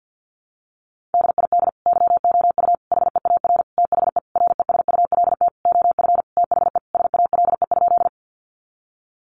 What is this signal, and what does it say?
Morse code, heard as a keyed tone, about 50 wpm, 700 Hz. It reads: DID YOU HEAR THE RESULT OF THE SALE?
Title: Sentences from Top 1400 Words 50wpm